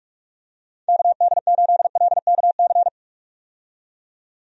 Morse code, heard as a keyed tone, about 43 wpm, 700 Hz. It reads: KD8LKC
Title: USA Call Signs - 6 Characters - 45wpm